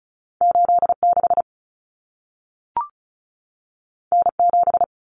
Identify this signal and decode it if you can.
Morse code, heard as a keyed tone, about 36 wpm, 700 Hz. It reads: 86 N7
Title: Common Cut Numbers - 2-Digit 35wpm